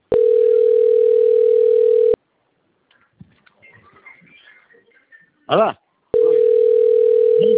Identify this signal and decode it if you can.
Ringback tone (North America), 440 + 480 Hz, with speech.
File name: out- - - - - .42880